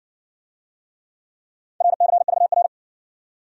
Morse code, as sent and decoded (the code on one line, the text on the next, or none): .-. -.-. ...- .-.
RCVR